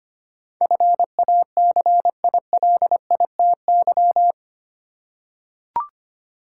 Morse code, heard as a keyed tone, about 25 wpm, 700 Hz. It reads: FACILITY
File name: Eight Letter Words - 25wpm